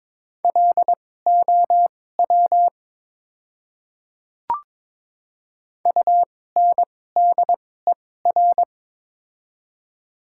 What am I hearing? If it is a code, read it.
Morse code, heard as a keyed tone, about 23 wpm, spaced as for 16 wpm, 700 Hz. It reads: LOW UNDER
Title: 2x Character Spacing - Top 200 Words - 22wpm